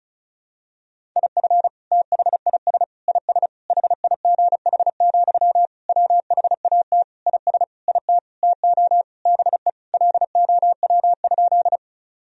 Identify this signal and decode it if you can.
Morse code, heard as a keyed tone, about 34 wpm, 700 Hz. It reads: IF THIS IS HIGH, WHAT IS IT TO BE LOW?
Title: Sentences from Top 1000 Words - Review Part 1 - 35wpm